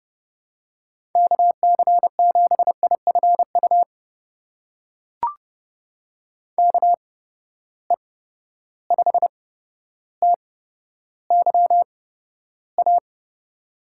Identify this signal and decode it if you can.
Morse code, heard as a keyed tone, about 30 wpm, spaced as for 13 wpm, 700 Hz. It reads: KC7IFU KE5TYA
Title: USA Call Signs - 6 Characters - 8x Character Spacing - 30wpm